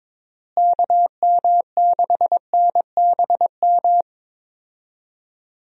Morse code, heard as a keyed tone, about 22 wpm, 700 Hz. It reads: KM6NBM
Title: USA Call Signs - 6 Characters - Repeated 2x - 22wpm